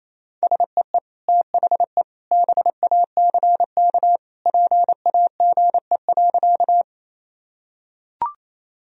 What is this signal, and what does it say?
Morse code, heard as a keyed tone, about 28 wpm, 700 Hz. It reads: SEE THE BACK PAGE.
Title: Sentences from Top 1000 Words - Review Part 1 - 28wpm